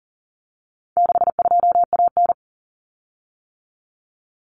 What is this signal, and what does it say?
Morse code, heard as a keyed tone, about 40 wpm, 700 Hz. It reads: 62AN
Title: Common Cut Numbers - 4-Digit 40wpm